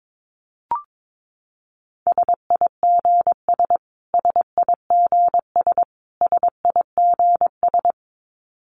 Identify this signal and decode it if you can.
Morse code, heard as a keyed tone, about 22 wpm, 700 Hz. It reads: SIGS SIGS SIGS
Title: Abbreviations Repeated 3x - 22wpm